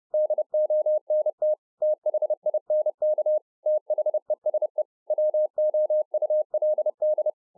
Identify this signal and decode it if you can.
Morse code, heard as a keyed tone, about 30 wpm, 610 Hz. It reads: DONT THINK THESE WOULD